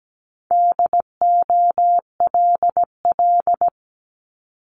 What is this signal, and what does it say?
Morse code, heard as a keyed tone, about 17 wpm, 700 Hz. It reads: DOLL
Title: Multi-letter Phonemes Repeated 3x 17wpm